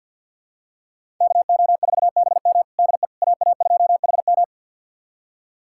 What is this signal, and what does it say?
Morse code, heard as a keyed tone, about 51 wpm, 700 Hz. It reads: KO4BM DE AA2HM
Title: USA Callsign DE Callsign - 5 Characters - 50wpm